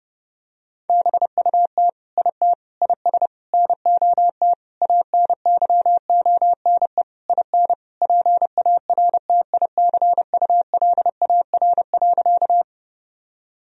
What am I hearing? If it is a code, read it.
Morse code, heard as a keyed tone, about 32 wpm, 700 Hz. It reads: BUT IT IS NOT ANYONE IN PARTICULAR.
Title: Sentences from Top 1000 Words - Review Part 4 - 30wpm